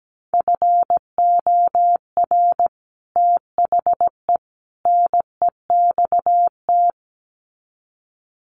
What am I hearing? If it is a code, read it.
Morse code, heard as a keyed tone, about 17 wpm, 700 Hz. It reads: FOR THE NEXT
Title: Sets of 3 Words 17wpm